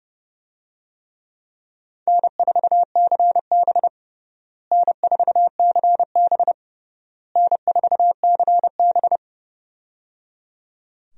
Morse code, heard as a keyed tone, about 30 wpm, 700 Hz. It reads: N4CB N4CB N4CB
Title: USA Call Signs - 4 Characters - Repeated 3x - 30wpm